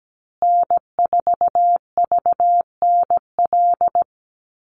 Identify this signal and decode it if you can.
Morse code, heard as a keyed tone, about 17 wpm, 700 Hz. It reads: N4VNL
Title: USA Call Signs Repeated 3x 17wpm